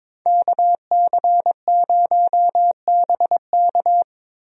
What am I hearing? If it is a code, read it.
Morse code, heard as a keyed tone, about 22 wpm, 700 Hz. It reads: KC0BK
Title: USA Call Signs - 5 Characters - Repeated 3x - 22wpm